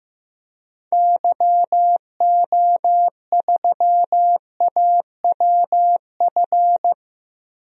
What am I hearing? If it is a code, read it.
Morse code, heard as a keyed tone, about 15 wpm, 700 Hz. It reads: YO3AWF
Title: DXCC-100 Call Signs - 6 Characters - 15wpm